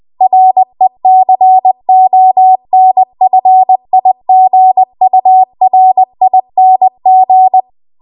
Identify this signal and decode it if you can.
Morse code, heard as a keyed tone, about 20 wpm, 750 Hz. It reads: RECONFIGURING
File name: Archive / 20 WPM / 20WPM